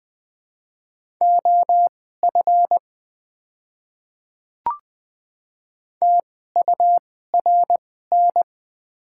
Morse code, heard as a keyed tone, about 21 wpm, spaced as for 15 wpm, 700 Hz. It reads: OF TURN
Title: 2x Character Spacing - Top 300 Words - 20wpm